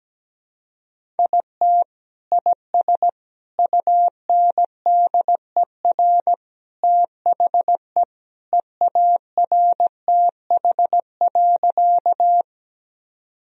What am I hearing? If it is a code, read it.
Morse code, heard as a keyed tone, about 17 wpm, 700 Hz. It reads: IT IS UNDER THE EARTH.
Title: Sentences from Top 1000 Words - Review Part 1 - 17wpm